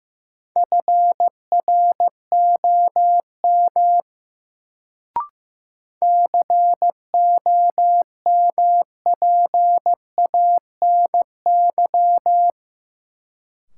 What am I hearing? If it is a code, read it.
Morse code, heard as a keyed tone, about 15 wpm, 700 Hz. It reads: FROM COMPANY
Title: Sets of 2 Words - New Word Drill - 15wpm